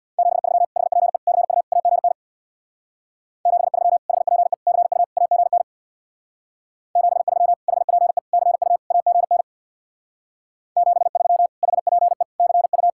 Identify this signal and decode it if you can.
Morse code, heard as a keyed tone, about 64 wpm, 700 Hz. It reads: 73 HPE CU AGN 73 HPE CU AGN 73 HPE CU AGN 73 HPE CU